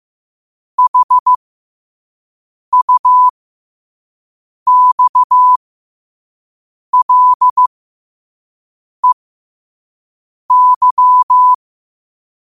Morse code, heard as a keyed tone, about 15 wpm, spaced as for 5 wpm, 1000 Hz. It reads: HUXLEY
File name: qotdfiles / 5wpm